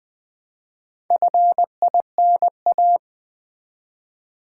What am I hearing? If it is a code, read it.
Morse code, heard as a keyed tone, about 20 wpm, 700 Hz. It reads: FINA
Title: Build A Word - 1 Letter At A Time - 3 to 5-Letter Words - 20wpm